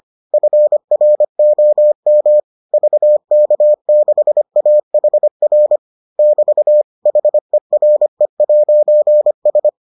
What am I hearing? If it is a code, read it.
Morse code, heard as a keyed tone, about 25 wpm, 600 Hz. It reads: FROM VK6AHR <BT> HERE'S